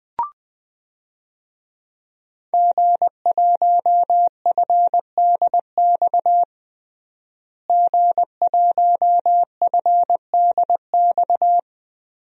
Morse code, heard as a keyed tone, about 20 wpm, 700 Hz. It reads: G1FDX G1FDX
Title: DXCC-100 Call Signs - 5 Characters - Repeated 3x - 20wpm